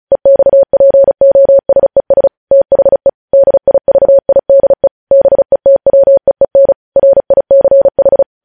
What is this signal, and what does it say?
Morse code, heard as a keyed tone, about 36 wpm, 550 Hz. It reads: EXPOSES THE DIVIDE BETWEEN RICH
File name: podcast / News headlines-1x-35x35